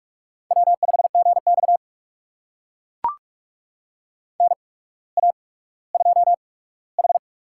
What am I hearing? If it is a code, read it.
Morse code, heard as a keyed tone, about 44 wpm, spaced as for 19 wpm, 700 Hz. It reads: W5GX NA2H
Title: USA Call Signs - 4 Characters - 8x Character Spacing - 45wpm